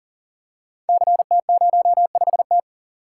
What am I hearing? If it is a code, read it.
Morse code, heard as a keyed tone, about 43 wpm, 700 Hz. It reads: CT05T
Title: Code Groups - 5 Letter:Numbers - 40wpm